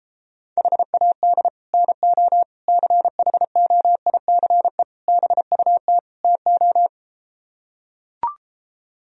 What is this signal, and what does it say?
Morse code, heard as a keyed tone, about 33 wpm, 700 Hz. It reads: HAD NO CHOICE BUT TO